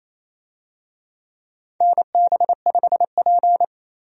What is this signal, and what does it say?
Morse code, heard as a keyed tone, about 28 wpm, 700 Hz. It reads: NB5P
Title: USA Call Signs - 4 Characters - 28wpm